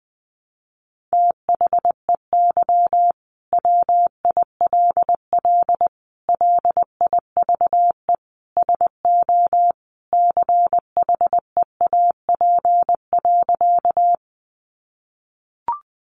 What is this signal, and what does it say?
Morse code, heard as a keyed tone, about 20 wpm, 700 Hz. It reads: THEY WILL LIVE SO CHEAP.
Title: Sentences from Top 1200 Words 20wpm